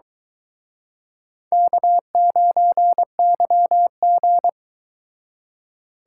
Morse code, heard as a keyed tone, about 24 wpm, 700 Hz. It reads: K9YG